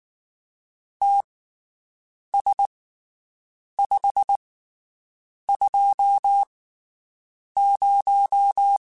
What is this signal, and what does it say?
Morse code, heard as a keyed tone, about 19 wpm, spaced as for 7 wpm, 800 Hz. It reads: TS520
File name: QSOs Spaced